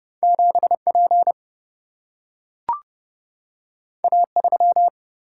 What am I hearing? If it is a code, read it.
Morse code, heard as a keyed tone, about 30 wpm, 700 Hz. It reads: 7 P A 3
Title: Single-Letter Single-Number 30wpm